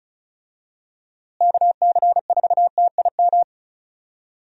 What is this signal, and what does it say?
Morse code, heard as a keyed tone, about 36 wpm, 700 Hz. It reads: KC4TIM
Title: USA Call Signs - 6 Characters - Repeated 3x - 35wpm